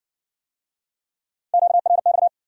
Morse code, heard as a keyed tone, about 56 wpm, 700 Hz. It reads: QRX